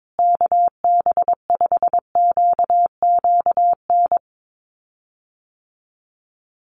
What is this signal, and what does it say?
Morse code, heard as a keyed tone, about 22 wpm, 700 Hz. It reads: KB5QQN
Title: USA Call Signs Repeated 3x 22wpm